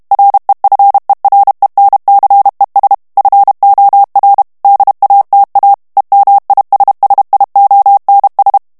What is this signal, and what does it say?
Morse code, heard as a keyed tone, about 32 wpm, 800 Hz. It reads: REFERENCES FOR DATA EMISSIONS
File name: ANGLAIS texte clair 3 vitesse 30 mots minute